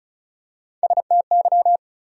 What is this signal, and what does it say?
Morse code, heard as a keyed tone, about 36 wpm, 700 Hz. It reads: STY
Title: USA Call Sign Postfixes - 3 Letters 35wpm